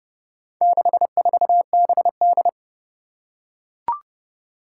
Morse code, heard as a keyed tone, about 30 wpm, 700 Hz. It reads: 64BD
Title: Four Challenging Letter-Number Combinations 30wpm